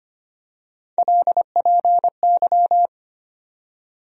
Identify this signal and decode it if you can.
Morse code, heard as a keyed tone, about 25 wpm, 700 Hz. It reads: LPY
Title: USA Call Sign Postfixes - 3 Letters 25wpm